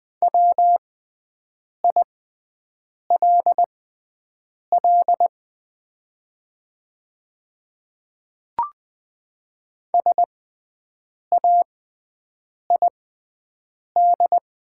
Morse code, heard as a keyed tone, about 20 wpm, spaced as for 7 wpm, 700 Hz. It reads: WILL SAID
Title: 6x Character Spacing - Top 200 Words - 20wpm